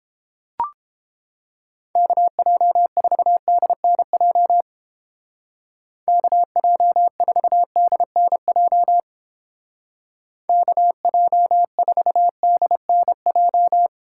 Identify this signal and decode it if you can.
Morse code, heard as a keyed tone, about 29 wpm, 700 Hz. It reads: KJ4DNJ KJ4DNJ KJ4DNJ